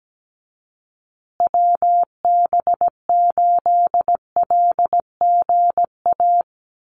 Morse code, heard as a keyed tone, about 17 wpm, 700 Hz. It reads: WB8LGA
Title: USA Call Signs Repeated 2x 17wpm